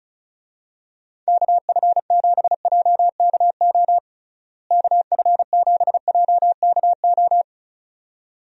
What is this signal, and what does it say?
Morse code, heard as a keyed tone, about 36 wpm, 700 Hz. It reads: KF7JKO KF7JKO
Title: USA Call Signs - 6 Characters - Repeated 2x - 35wpm